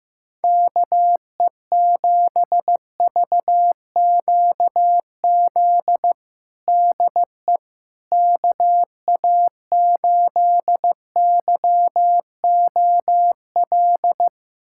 Morse code, heard as a keyed tone, about 15 wpm, 700 Hz. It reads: KE7VQZ DE KA8YOL
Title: USA Callsign DE Callsign - 6 Characters - 15wpm